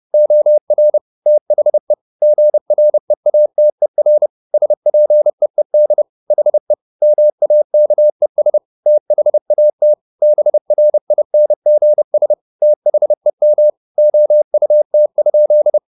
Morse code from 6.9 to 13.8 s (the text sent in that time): MAKES THAT BRINGS THEM